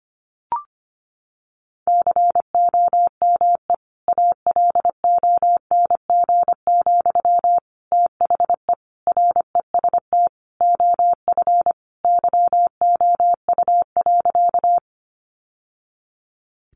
Morse code, heard as a keyed tone, about 25 wpm, 700 Hz. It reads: COME ALONG, THE REST OF YOU.
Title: Sentences from Top 700 Words 25wpm